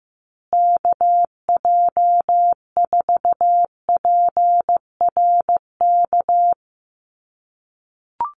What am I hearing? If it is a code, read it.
Morse code, heard as a keyed tone, about 14 wpm, 700 Hz. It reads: KJ4PRK
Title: USA Call Signs Repeated 3x 15wpm